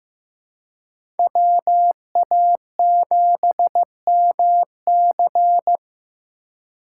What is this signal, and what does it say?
Morse code, heard as a keyed tone, about 18 wpm, 700 Hz. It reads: WA7MC
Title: Speed Racing - USA 2x2 Call Signs 15wpm